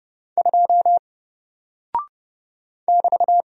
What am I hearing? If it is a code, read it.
Morse code, heard as a keyed tone, about 30 wpm, 700 Hz. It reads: 2 <BT>
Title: Single Character (Letter:Number:Punctuation:Prosign) Alt 30wpm